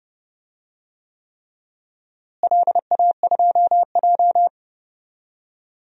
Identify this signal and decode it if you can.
Morse code, heard as a keyed tone, about 30 wpm, 700 Hz. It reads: LA2J
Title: DXCC-100 Call Signs - 4 Characters - Repeated 3x - 30wpm